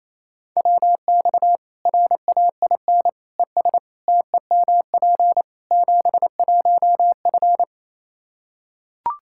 Morse code, heard as a keyed tone, about 28 wpm, 700 Hz. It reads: WX RAIN ES TEMP 71F
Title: K1USN SST QSO Practice 28wpm